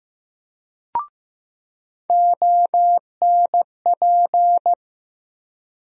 Morse code, heard as a keyed tone, about 15 wpm, 700 Hz. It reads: ONP
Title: USA Call Sign Postfixes - 3 Letters 15wpm